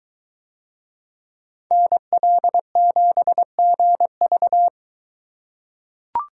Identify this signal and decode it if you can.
Morse code, heard as a keyed tone, about 23 wpm, 700 Hz. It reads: NL7GV